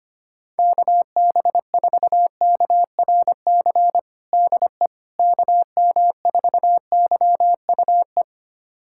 Morse code, heard as a keyed tone, about 25 wpm, 700 Hz. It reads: KB4KRC DE KM4YUE
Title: USA Callsign DE Callsign - 6 Characters - 25wpm